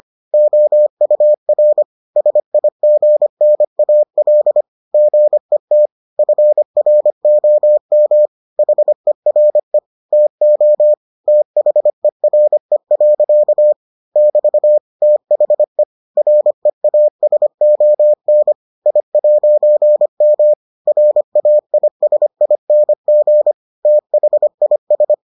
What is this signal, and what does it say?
Morse code, heard as a keyed tone, about 25 wpm, 600 Hz. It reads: OUR SIGNAL GET FROM HERE TO THERE. <BT> THE REASON I'M RAISING THIS